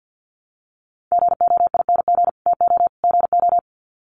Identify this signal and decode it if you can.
Morse code, heard as a keyed tone, about 48 wpm, 700 Hz. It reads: GOING TO GO
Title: Sets of 3 Words 50wpm